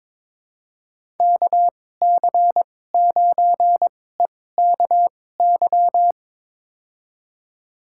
Morse code, heard as a keyed tone, about 22 wpm, spaced as for 16 wpm, 700 Hz. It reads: KC9EKY
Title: USA Call Signs - 6 Characters - 2x Character Spacing - 22wpm